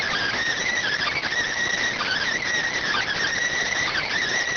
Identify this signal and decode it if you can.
Emergency Alert System (SAME) tones, heard as harsh buzzing bursts. It is heard from 0.6 to 4.1 s.